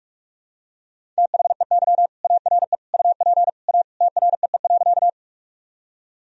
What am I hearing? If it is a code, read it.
Morse code, heard as a keyed tone, about 47 wpm, 700 Hz. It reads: THEY ARE UP A TREE.
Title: Sentences from Top 1000 Words - Review Part 1 - 45wpm